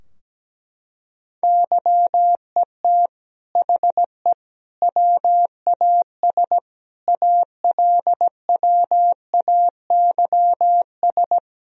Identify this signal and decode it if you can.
Morse code, heard as a keyed tone, about 17 wpm, 700 Hz. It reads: YET HE WAS ALWAYS
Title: Sentences from Top 1000 Words - Review Part 3 - 17wpm